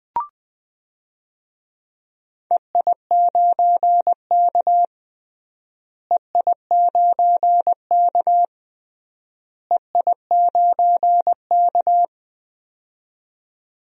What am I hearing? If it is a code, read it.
Morse code, heard as a keyed tone, about 20 wpm, 700 Hz. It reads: EI9K EI9K EI9K
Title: DXCC-100 Call Signs - 4 Characters - Repeated 3x - 20wpm